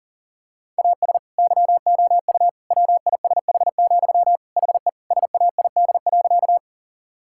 Morse code, heard as a keyed tone, about 40 wpm, 700 Hz. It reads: AS YOU WISH, HE SAID.
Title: Sentences from Top 1600 Words 40wpm